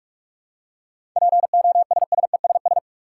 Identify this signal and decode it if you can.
Morse code, heard as a keyed tone, about 47 wpm, 700 Hz. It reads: POSSESS
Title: Seven Letter Words - 45wpm